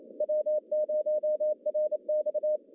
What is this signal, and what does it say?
Morse code, heard as a keyed tone, about 27 wpm, 610 Hz. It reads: W0RX
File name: CW / cqww-cw-k5tr